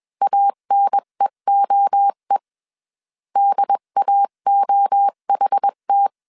Morse code, heard as a keyed tone, about 21 wpm, 800 Hz. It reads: ANEOE DAOHT
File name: 20wpm